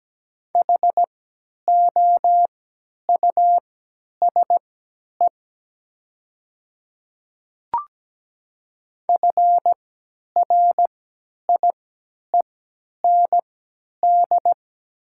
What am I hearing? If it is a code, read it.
Morse code, heard as a keyed tone, about 17 wpm, spaced as for 10 wpm, 700 Hz. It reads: HOUSE FRIEND
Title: 3x Character Spacing - Top 300 Words - 17wpm